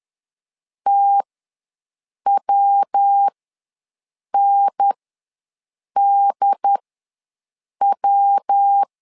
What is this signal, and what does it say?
Morse code, heard as a keyed tone, about 10 wpm, spaced as for 6 wpm, 800 Hz. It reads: TWNDW
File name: LEZIONI / 3 MWRO / 3 MW ws 10wpm 2213